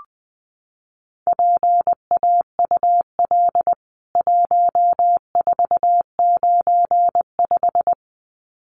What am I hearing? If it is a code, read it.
Morse code, heard as a keyed tone, about 20 wpm, 700 Hz. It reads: PAUL 1495
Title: CWT Exchange 20wpm